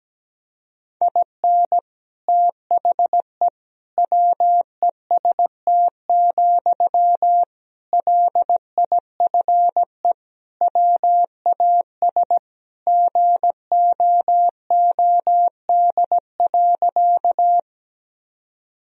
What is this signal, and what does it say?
Morse code, heard as a keyed tone, about 17 wpm, 700 Hz. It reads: IN THE WEST, LIFE WAS GOOD.
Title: Sentences from Top 1000 Words - Review Part 2 - 17wpm